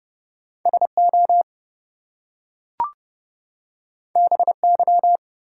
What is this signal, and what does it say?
Morse code, heard as a keyed tone, about 30 wpm, 700 Hz. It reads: SO BY
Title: Two Letter Words - 30wpm